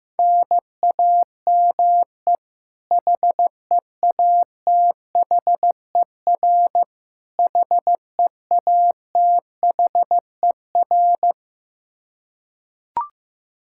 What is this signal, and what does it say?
Morse code, heard as a keyed tone, about 15 wpm, 700 Hz. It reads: NAME HEATHER HEATHER
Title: K1USN SST QSO Practice 15wpm